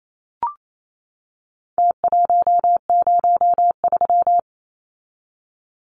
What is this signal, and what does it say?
Morse code, heard as a keyed tone, about 28 wpm, 700 Hz. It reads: T103
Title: Numbers - Contest Style - 4-Digit 28wpm